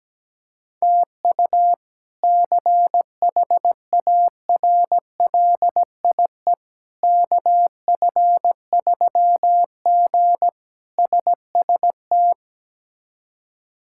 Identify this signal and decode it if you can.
Morse code, heard as a keyed tone, about 17 wpm, 700 Hz. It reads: TU CHARLIE KF3G SST
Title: K1USN SST QSO Practice - 17wpm